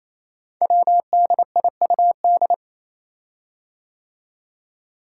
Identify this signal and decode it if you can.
Morse code, heard as a keyed tone, about 27 wpm, 700 Hz. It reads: WDIUD
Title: Code Groups - 5 Letter:Numbers - 28wpm